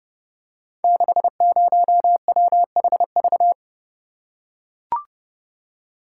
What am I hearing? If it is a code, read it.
Morse code, heard as a keyed tone, about 30 wpm, 700 Hz. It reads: 60WHV
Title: Code Groups - 5 Letter:Numbers - 30wpm